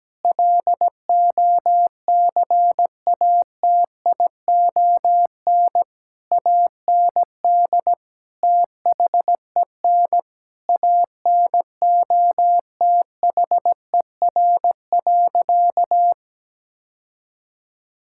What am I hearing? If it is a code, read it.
Morse code, heard as a keyed tone, about 17 wpm, 700 Hz. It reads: LOCATION AND THEN ANOTHER.
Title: Sentences from Top 1000 Words - Review Part 6 - 17wpm